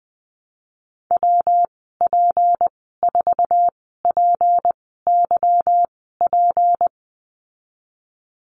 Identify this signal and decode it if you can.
Morse code, heard as a keyed tone, about 20 wpm, spaced as for 15 wpm, 700 Hz. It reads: WP4PYP
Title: 186 Lesson 30g - Letter b - Call Signs 20wpm